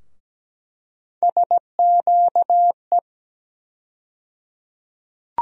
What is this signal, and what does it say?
Morse code, heard as a keyed tone, about 17 wpm, 700 Hz. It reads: SQE
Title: USA Call Sign Postfixes - 3 Letters 17wpm